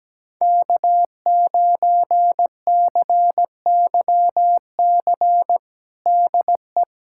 Morse code, heard as a keyed tone, about 17 wpm, 700 Hz. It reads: K9CYC DE
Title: USA Callsign DE Callsign - 5 Characters - 17wpm